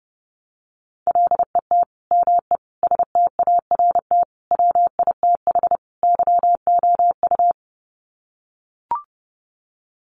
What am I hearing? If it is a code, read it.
Morse code, heard as a keyed tone, about 30 wpm, 700 Hz. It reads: LET ME START WITH YOU
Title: Sets of 5 Words Encore 30wpm